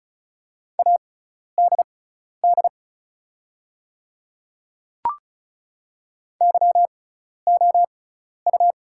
Morse code, heard as a keyed tone, about 34 wpm, spaced as for 12 wpm, 690 Hz. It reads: ADD YOU
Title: 6x Character Spacing - Top 200 Words - 35wpm